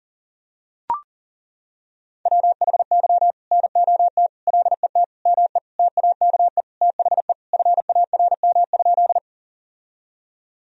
Morse code, heard as a keyed tone, about 37 wpm, 700 Hz. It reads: WHY NOT LET ME TAKE THE FARM?
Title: Sentences from Top 1000 Words - Review Part 2 - 40wpm